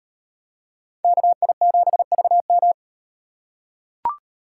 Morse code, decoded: KI7VM

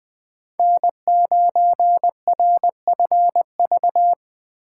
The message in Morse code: N9RFV

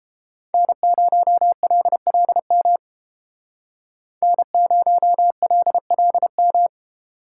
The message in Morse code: N0LLM N0LLM

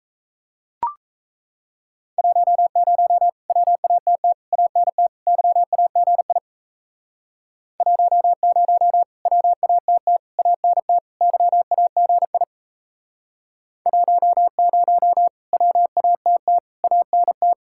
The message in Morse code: 10 WATT ANT YAGI 10 WATT ANT YAGI 10 WATT ANT

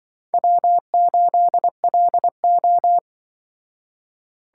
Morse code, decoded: W8LO

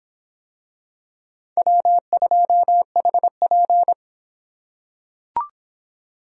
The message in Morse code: W2HP

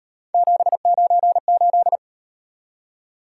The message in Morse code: 798

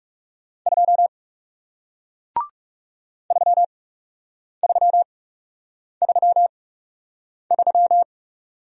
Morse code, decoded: 2 3333

